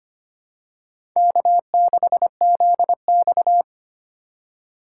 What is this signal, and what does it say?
Morse code, heard as a keyed tone, about 25 wpm, 700 Hz. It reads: K6ZX
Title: K1USN SST QSO Practice - 25wpm